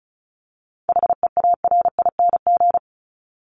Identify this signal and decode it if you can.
Morse code, heard as a keyed tone, about 34 wpm, 700 Hz. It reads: HEARING
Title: Top 1751-2000 Words 35wpm